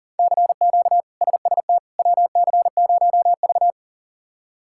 Morse code, heard as a keyed tone, about 40 wpm, 690 Hz. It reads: CQ SST WC0V
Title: K1USN SST QSO Practice - 40wpm